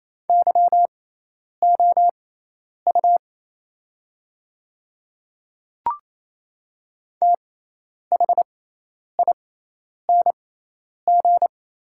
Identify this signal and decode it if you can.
Morse code, heard as a keyed tone, about 27 wpm, spaced as for 10 wpm, 690 Hz. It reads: YOU THING
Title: 6x Character Spacing - Top 100 Words - 28wpm